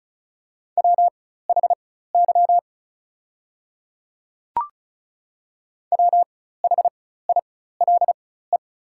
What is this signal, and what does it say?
Morse code, heard as a keyed tone, about 34 wpm, spaced as for 16 wpm, 700 Hz. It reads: WHY WHILE
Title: 4x Character Spacing - Top 300 Words - 35wpm